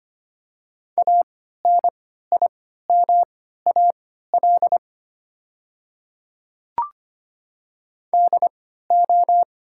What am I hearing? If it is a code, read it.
Morse code, heard as a keyed tone, about 25 wpm, spaced as for 14 wpm, 700 Hz. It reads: ANIMAL DO
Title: 3x Character Spacing - Top 200 Words - 25wpm